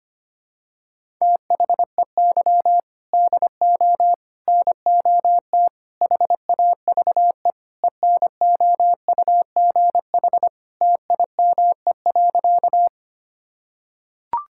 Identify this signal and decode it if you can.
Morse code, heard as a keyed tone, about 25 wpm, 700 Hz. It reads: THEY DO NOT HAVE ENOUGH TIME.
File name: practice / 25wpm / Sentences with 4-Word Phrases - 25wpm